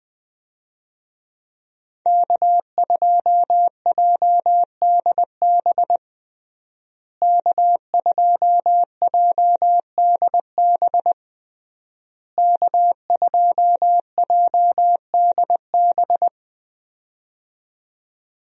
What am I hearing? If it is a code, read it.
Morse code, heard as a keyed tone, about 20 wpm, 700 Hz. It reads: K2JDB K2JDB K2JDB
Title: USA Call Signs - 5 Characters - Repeated 3x - 20wpm